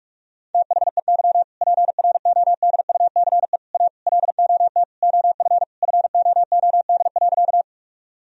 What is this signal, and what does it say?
Morse code, heard as a keyed tone, about 42 wpm, 700 Hz. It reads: THEY PRODUCE A LOT OF FOOD.
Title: Sentences from Top 1000 Words - Review Part 3 - 45wpm